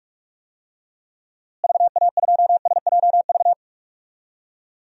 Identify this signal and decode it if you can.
Morse code, heard as a keyed tone, about 42 wpm, 700 Hz. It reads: VA2SJV
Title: DXCC-100 Call Signs - 6 Characters - Repeated 2x - 45wpm